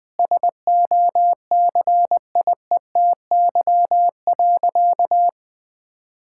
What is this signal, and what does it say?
Morse code, heard as a keyed tone, about 20 wpm, 700 Hz. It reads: SOCIETY.
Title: Sentences from Top 1600 Words 20wpm